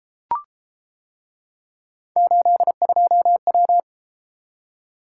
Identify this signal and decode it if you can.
Morse code, heard as a keyed tone, about 31 wpm, 700 Hz. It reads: 82W